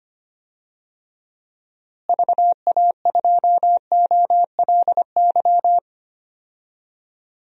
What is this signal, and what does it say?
Morse code, heard as a keyed tone, about 25 wpm, 700 Hz. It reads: VA2OLY
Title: DXCC-100 Call Signs - 6 Characters - Repeated 2x - 25wpm